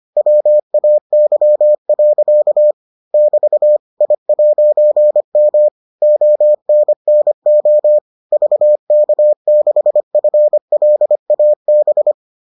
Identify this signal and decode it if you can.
Morse code, heard as a keyed tone, about 25 wpm, 600 Hz. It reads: WAY. <BT> I'M ONNO VK6FLAB